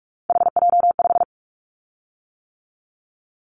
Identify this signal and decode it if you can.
Morse code, heard as a keyed tone, about 46 wpm, 700 Hz. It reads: HJ5
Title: Callsign Prefixes for Top 100 DXCC - 45wpm